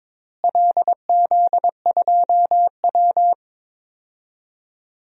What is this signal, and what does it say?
Morse code, heard as a keyed tone, about 22 wpm, 700 Hz. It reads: LZ2W
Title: DXCC-100 Call Signs - 4 Characters - Repeated 2x - 22wpm